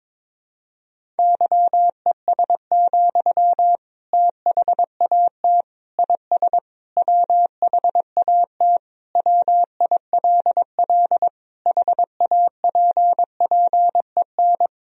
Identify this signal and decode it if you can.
Morse code, heard as a keyed tone, about 21 wpm, 700 Hz. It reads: YES, THAT IS WHAT WILL HAPPEN
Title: Sentences from Top 500 Words and Abbreviations 22wpm